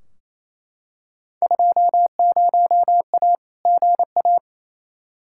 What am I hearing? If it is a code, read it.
Morse code, heard as a keyed tone, about 28 wpm, 700 Hz. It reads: 20A GA
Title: Field Day Exchanges Encore - 28wpm